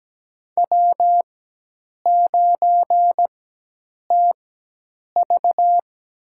Morse code, heard as a keyed tone, about 17 wpm, spaced as for 8 wpm, 700 Hz. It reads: W9TV
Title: USA Call Signs - 4 Characters - 4x Character Spacing - 17wpm